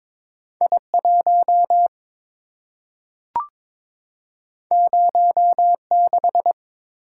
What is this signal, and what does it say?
Morse code, heard as a keyed tone, about 23 wpm, 700 Hz. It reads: I1 06
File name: practice / 22wpm / Two Letter-Number Combinations - 22wpm